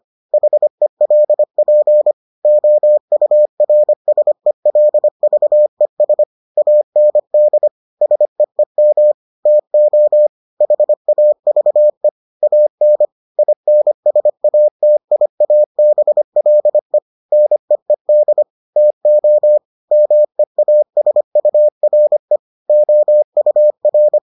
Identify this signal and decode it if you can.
Morse code, heard as a keyed tone, about 25 wpm, 600 Hz. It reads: HELP OURSELVES AND SEEM TO HAVE AN INSATIABLE NEED TO MEASURE OUR